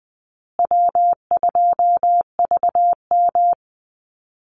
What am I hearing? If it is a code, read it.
Morse code, heard as a keyed tone, about 20 wpm, 700 Hz. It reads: W2VM
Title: Easy Call Signs Repeated 3x - 20wpm